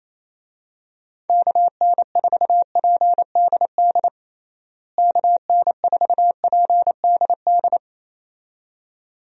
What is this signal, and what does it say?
Morse code, heard as a keyed tone, about 27 wpm, 700 Hz. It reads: KN4PDD KN4PDD
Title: USA Call Signs - 6 Characters - Repeated 2x - 28wpm